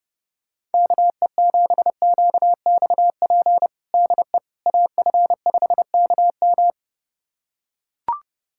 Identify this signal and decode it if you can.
Morse code, heard as a keyed tone, about 28 wpm, 700 Hz. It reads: KE7QXP DE AF5KM
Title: USA Callsign DE Callsign 30wpm